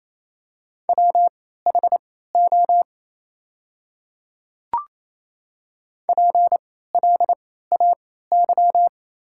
Morse code, heard as a keyed tone, about 28 wpm, spaced as for 16 wpm, 700 Hz. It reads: WHO PLAY
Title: 3x Character Spacing - Top 200 Words - 28wpm